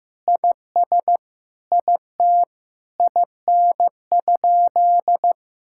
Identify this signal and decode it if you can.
Morse code, heard as a keyed tone, about 15 wpm, 700 Hz. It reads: IS IT IN?
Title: Sentences from Top 1600 Words 15wpm